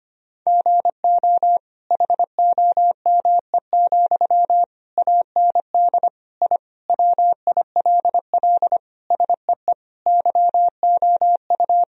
Morse code, heard as a keyed tone, about 25 wpm, 700 Hz. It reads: GO HOME, AND I WILL SEE YOU
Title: Sentences from Top 1000 Words - Review Part 4 - 25wpm